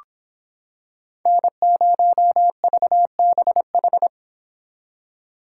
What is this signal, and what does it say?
Morse code, heard as a keyed tone, about 26 wpm, 700 Hz. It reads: N0VBH